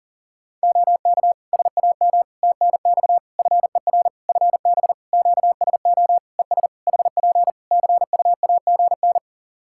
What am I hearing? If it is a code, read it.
Morse code, heard as a keyed tone, about 40 wpm, 700 Hz. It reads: OK SAM TNX FER FB QSO ES HP CUAGN
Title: Practice QSO - Exchange 4 - Ending - 40wpm